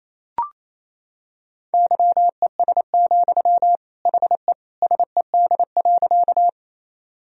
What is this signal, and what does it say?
Morse code, heard as a keyed tone, about 28 wpm, 700 Hz. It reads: YES, HE SED.
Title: Sentences from Top 500 Words and Abbreviations 28wpm